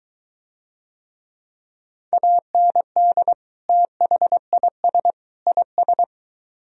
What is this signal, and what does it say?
Morse code, heard as a keyed tone, about 23 wpm, 700 Hz. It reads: AND THIS IS